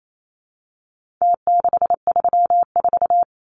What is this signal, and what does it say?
Morse code, heard as a keyed tone, about 28 wpm, 700 Hz. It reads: T634
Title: Numbers - Contest Style - 4-Digit 28wpm